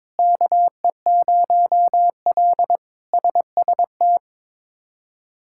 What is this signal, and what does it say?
Morse code, heard as a keyed tone, about 22 wpm, 700 Hz. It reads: KE0L SST
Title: K1USN SST QSO Practice - 22wpm